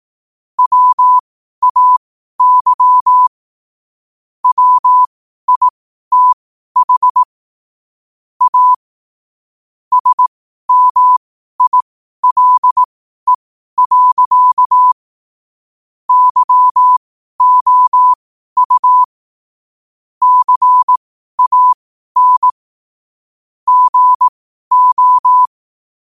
Morse code, heard as a keyed tone, about 18 wpm, spaced as for 12 wpm, 1000 Hz. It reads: WAY WITH A SMILE. YOU CAN GO
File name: qotdfiles / 12wpm